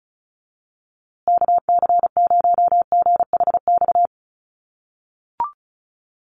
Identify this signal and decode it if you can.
Morse code, heard as a keyed tone, about 34 wpm, 700 Hz. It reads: KC0GHX
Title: USA Call Signs Repeated 3x 35wpm